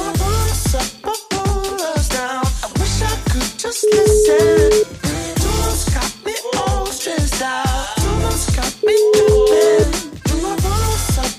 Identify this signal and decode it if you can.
Ringback tone (Europe, 425 Hz), with noise.